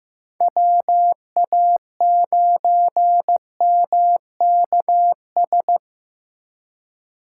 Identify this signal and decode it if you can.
Morse code, heard as a keyed tone, about 18 wpm, 700 Hz. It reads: WA9MKS
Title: Speed Racing - USA 2x3 Call Signs 15wpm